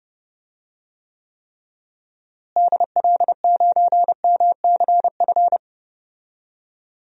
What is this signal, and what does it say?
Morse code, heard as a keyed tone, about 32 wpm, 700 Hz. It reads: DL9MCF
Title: DXCC-100 Call Signs - 6 Characters - Repeated 3x - 30wpm